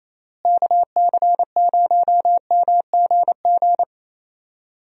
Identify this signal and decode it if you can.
Morse code, heard as a keyed tone, about 28 wpm, 700 Hz. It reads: KC0MGG
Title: USA Call Signs - 6 Characters - Repeated 2x - 28wpm